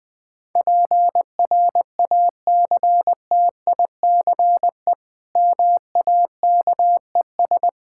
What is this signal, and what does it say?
Morse code, heard as a keyed tone, about 21 wpm, 700 Hz. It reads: PRACTICE MAKES
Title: Difficult Sentences with Hints - 20wpm